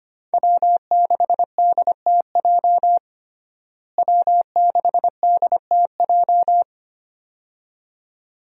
Morse code, heard as a keyed tone, about 25 wpm, 700 Hz. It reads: W6DTJ W6DTJ
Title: USA Call Signs - 5 Characters - Repeated 2x - 25wpm